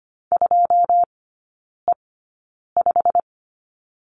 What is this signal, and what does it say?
Morse code, heard as a keyed tone, about 25 wpm, spaced as for 9 wpm, 700 Hz. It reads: 2E5
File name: Lesson 15e - Number 5 - Three Characters 25wpm